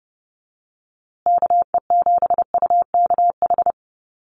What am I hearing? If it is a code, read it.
Morse code, heard as a keyed tone, about 30 wpm, 700 Hz. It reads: KE7UKH
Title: USA Call Signs Repeated 3x 30wpm